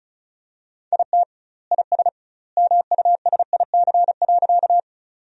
Morse code, heard as a keyed tone, about 36 wpm, 700 Hz. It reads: IT IS MUSIC.
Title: 2x Word-Spacing - Sentences from Top 300 Words - 35wpm